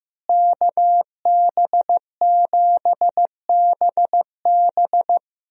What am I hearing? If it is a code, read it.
Morse code, heard as a keyed tone, about 15 wpm, 700 Hz. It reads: KB7BB
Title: USA Call Signs - 5 Characters - Repeated 2x - 15wpm